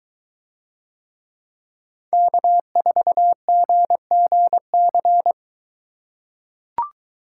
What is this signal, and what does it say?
Morse code, heard as a keyed tone, about 22 wpm, 700 Hz. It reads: K4GGC